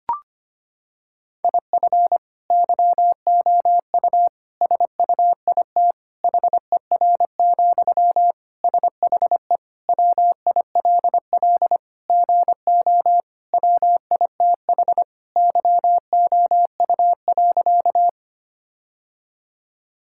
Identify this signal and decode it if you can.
Morse code, heard as a keyed tone, about 25 wpm, 700 Hz. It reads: IF YOU SUIT HER, SHE WILL GO WITH YOU.
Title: Sentences from Top 1600 Words 25wpm